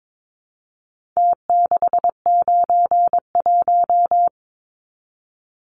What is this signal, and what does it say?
Morse code, heard as a keyed tone, about 22 wpm, 700 Hz. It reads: T691
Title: Numbers - Contest Style - 4-Digit 22wpm